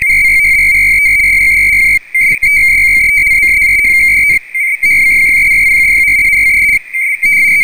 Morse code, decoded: none